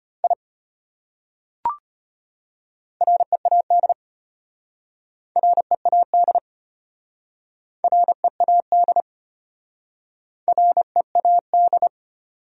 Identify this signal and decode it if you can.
Morse code, heard as a keyed tone, about 32 wpm, 700 Hz. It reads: I READ READ READ READ